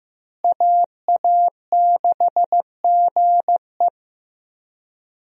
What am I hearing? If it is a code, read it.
Morse code, heard as a keyed tone, about 15 wpm, 700 Hz. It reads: AA6GE
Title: USA Call Signs - 5 Characters - Repeated 3x - 15wpm